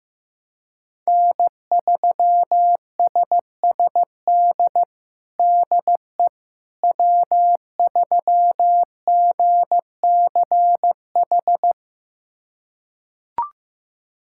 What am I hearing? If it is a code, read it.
Morse code, heard as a keyed tone, about 16 wpm, 700 Hz. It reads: N3SSD DE W3GCH
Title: USA Callsign DE Callsign - 5 Characters - 15wpm